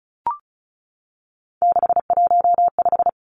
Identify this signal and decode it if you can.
Morse code, heard as a keyed tone, about 34 wpm, 700 Hz. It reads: 615
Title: Numbers - 3-Digit 35wpm